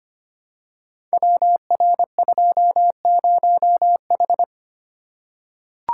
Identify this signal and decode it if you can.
Morse code, heard as a keyed tone, about 25 wpm, 700 Hz. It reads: WR20H
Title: Code Groups - 5 Letter:Numbers - 25wpm